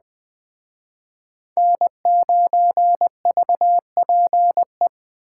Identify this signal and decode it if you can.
Morse code, heard as a keyed tone, about 21 wpm, 700 Hz. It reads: N9VPE
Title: Speed Racing - USA 1x3 Call Signs 17wpm